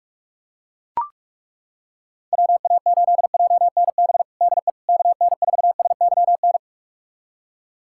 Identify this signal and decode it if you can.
Morse code, heard as a keyed tone, about 42 wpm, 700 Hz. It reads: WA8JNB DE KN4SYN
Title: USA Callsign DE Callsign - 6 Characters - 45wpm